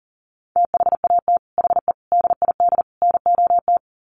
Morse code, heard as a keyed tone, about 37 wpm, 700 Hz. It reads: THAT HE DID NOT
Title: Sets of 4 Words 40wpm